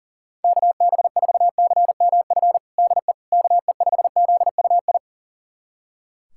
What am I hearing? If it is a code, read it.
Morse code, heard as a keyed tone, about 37 wpm, 700 Hz. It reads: KB4CMF DE KE5ZUI
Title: USA Callsign DE Callsign 40wpm